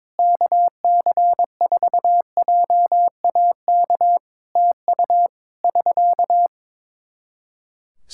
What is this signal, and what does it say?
Morse code, heard as a keyed tone, about 22 wpm, 700 Hz. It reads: KC4JAK TU <SK>
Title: Practice QSO - Exchange 4 - Ending - 22wpm